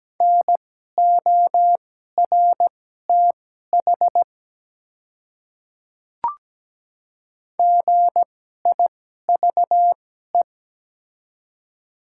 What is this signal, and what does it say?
Morse code, heard as a keyed tone, about 17 wpm, spaced as for 12 wpm, 700 Hz. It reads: NORTH GIVE
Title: 2x Character Spacing - Top 300 Words - 17wpm